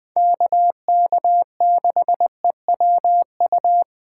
Morse code, heard as a keyed tone, about 23 wpm, 700 Hz. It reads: KK6EWU
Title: Speed Racing - USA 2x3 Call Signs 20wpm